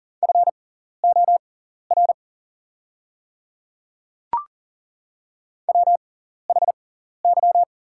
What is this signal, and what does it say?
Morse code, heard as a keyed tone, about 40 wpm, spaced as for 14 wpm, 700 Hz. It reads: FOR WHY
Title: 6x Character Spacing - Top 200 Words - 40wpm